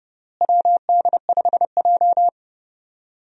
Morse code, heard as a keyed tone, about 29 wpm, 700 Hz. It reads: WD5J
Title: USA Call Signs - 4 Characters - Repeated 2x - 30wpm